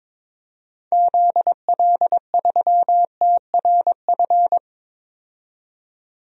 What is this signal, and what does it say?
Morse code, heard as a keyed tone, about 22 wpm, 700 Hz. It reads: ZL3TRF
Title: DXCC-100 Call Signs - 6 Characters - 22wpm